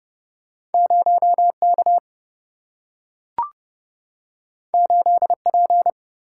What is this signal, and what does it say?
Morse code, heard as a keyed tone, about 30 wpm, 700 Hz. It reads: 0K 8P
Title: Single-Letter Single-Number 30wpm